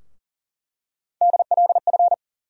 Morse code, heard as a keyed tone, about 40 wpm, 700 Hz. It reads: DLF
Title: USA Call Sign Postfixes - 3 Letters 40wpm